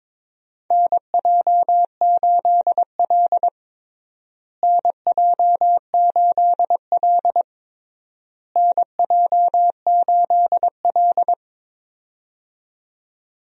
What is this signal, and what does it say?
Morse code, heard as a keyed tone, about 22 wpm, 700 Hz. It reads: NJ8L NJ8L NJ8L
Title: USA Call Signs - 4 Characters - Repeated 3x - 22wpm